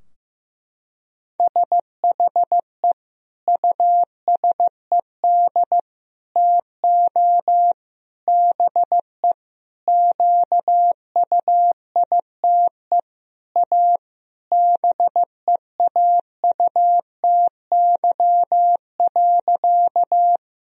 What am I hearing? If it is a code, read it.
Morse code, heard as a keyed tone, about 15 wpm, 700 Hz. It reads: SHE USED TO BE QUITE A BEAUTY.
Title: Sentences from Top 1000 Words - Review Part 2 - 15wpm